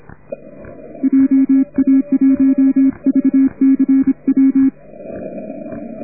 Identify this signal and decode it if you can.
Morse code, heard as a keyed tone, about 26 wpm, 270 Hz. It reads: JA1VCW